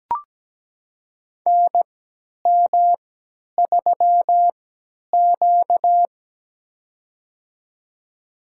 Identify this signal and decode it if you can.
Morse code, heard as a keyed tone, about 17 wpm, spaced as for 10 wpm, 700 Hz. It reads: NM3Q
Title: USA Call Signs - 4 Characters - 3x Character Spacing - 17wpm